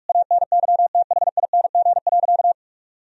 Morse code, heard as a keyed tone, about 41 wpm, 700 Hz. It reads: ANYTHING.
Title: Sentences from Top 1000 Words - Review Part 3 - 45wpm